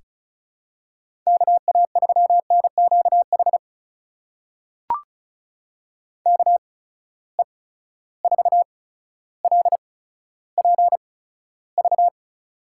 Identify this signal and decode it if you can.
Morse code, heard as a keyed tone, about 34 wpm, spaced as for 15 wpm, 700 Hz. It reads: KA3NQH KE4LPV
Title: USA Call Signs - 6 Characters - 8x Character Spacing - 35wpm